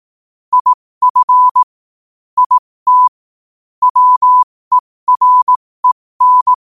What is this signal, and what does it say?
Morse code, heard as a keyed tone, about 18 wpm, spaced as for 15 wpm, 1000 Hz. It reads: IF IT WEREN
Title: 15 WPM morse code quotes for Thu, 29 Feb 2024